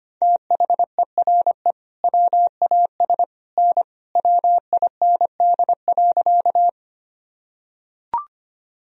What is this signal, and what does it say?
Morse code, heard as a keyed tone, about 25 wpm, 700 Hz. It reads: THERE WAS N WIND.
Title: Sentences from Top 400 Words and Abbreviations 25wpm